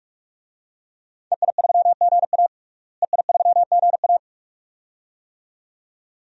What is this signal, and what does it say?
Morse code, heard as a keyed tone, about 41 wpm, 700 Hz. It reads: EI3GA EI3GA
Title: DXCC-100 Call Signs - 5 Characters - Repeated 2x - 45wpm